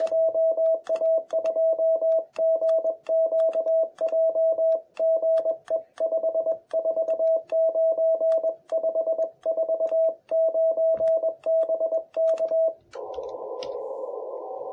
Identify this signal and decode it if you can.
Morse code, heard as a keyed tone, about 21 wpm, 650 Hz. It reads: JA2GQJ GE 549 549 BK
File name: jh8cbh / 2021cw